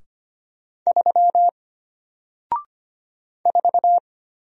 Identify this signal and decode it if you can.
Morse code, heard as a keyed tone, about 25 wpm, 700 Hz. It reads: 3 4
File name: Single-Letter-Number-25wpm